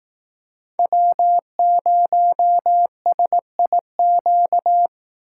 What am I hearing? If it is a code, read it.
Morse code, heard as a keyed tone, about 18 wpm, 700 Hz. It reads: W0SIQ
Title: Speed Racing - USA 1x3 Call Signs 15wpm